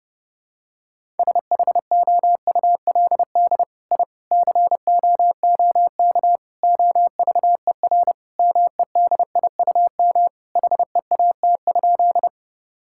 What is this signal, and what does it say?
Morse code, heard as a keyed tone, about 30 wpm, 700 Hz. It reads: SHOULD I COOK OVER MEDIUM HEAT?
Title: Sentences from Top 1000 Words - Review Part 6 - 30wpm